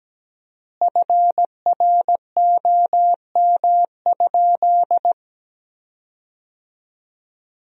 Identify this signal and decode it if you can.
Morse code, heard as a keyed tone, about 17 wpm, 700 Hz. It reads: FROM?
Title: 4x Word-Spacing - Sentences from Top 300 Words - 17wpm